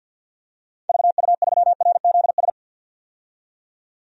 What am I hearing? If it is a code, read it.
Morse code, heard as a keyed tone, about 47 wpm, 700 Hz. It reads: VU3RZS
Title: DXCC-100 Call Signs - 6 Characters - 50wpm